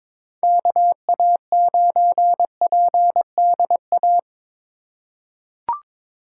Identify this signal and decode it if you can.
Morse code, heard as a keyed tone, about 22 wpm, 700 Hz. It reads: KA9PDA
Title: USA Call Signs - 6 Characters - 3x Character Spacing - 22wpm